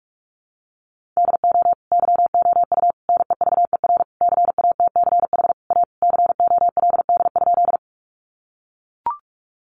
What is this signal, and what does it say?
Morse code, heard as a keyed tone, about 44 wpm, 700 Hz. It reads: DO YOU NEVER CATCH A COLD?
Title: Sentences from Top 1500 Words 45wpm